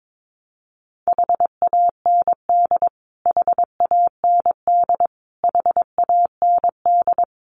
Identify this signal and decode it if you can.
Morse code, heard as a keyed tone, about 22 wpm, 700 Hz. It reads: HAND HAND HAND
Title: Top 200 Words Repeated 22wpm